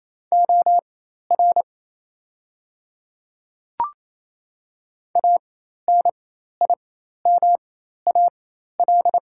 Morse code, heard as a keyed tone, about 28 wpm, spaced as for 13 wpm, 700 Hz. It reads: OR ANIMAL
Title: 4x Character Spacing - Top 200 Words - 28wpm